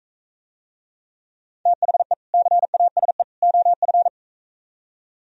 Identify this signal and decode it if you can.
Morse code, heard as a keyed tone, about 45 wpm, 700 Hz. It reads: THE CASE OF